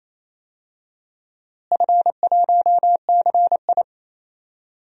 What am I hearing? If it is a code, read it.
Morse code, heard as a keyed tone, about 28 wpm, 700 Hz. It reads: F1CI
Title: DXCC-100 Call Signs - 4 Characters - Repeated 3x - 28wpm